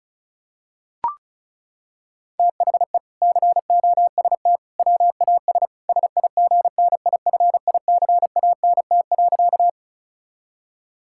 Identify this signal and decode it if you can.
Morse code, heard as a keyed tone, about 34 wpm, 700 Hz. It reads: THE COST WAS SIGNIFICANT.
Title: Sentences from Top 1000 Words - Review Part 6 - 35wpm